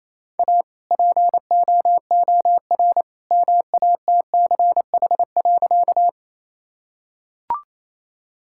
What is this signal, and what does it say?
Morse code, heard as a keyed tone, about 27 wpm, 700 Hz. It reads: A POOR MATCH.
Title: Sentences from Top 1600 Words 28wpm